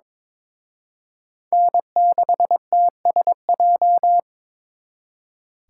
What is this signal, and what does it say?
Morse code, heard as a keyed tone, about 26 wpm, 700 Hz. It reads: N6TSJ
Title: Speed Racing - USA 1x3 Call Signs 22wpm